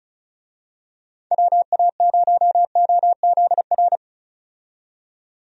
Morse code, heard as a keyed tone, about 36 wpm, 700 Hz. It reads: WA0OZR
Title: USA Call Signs - 6 Characters - 35wpm